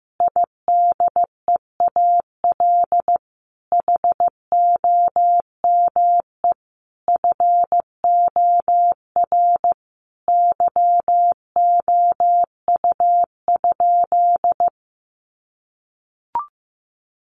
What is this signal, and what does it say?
Morse code, heard as a keyed tone, about 15 wpm, 700 Hz. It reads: IDEAL HOME FOR YOU?
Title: Sentences from Top 1400 Words 15wpm